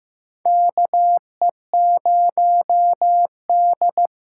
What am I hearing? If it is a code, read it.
Morse code, heard as a keyed tone, about 15 wpm, 700 Hz. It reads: KE0D
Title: USA Call Signs - 6 Characters - 8x Character Spacing - 15wpm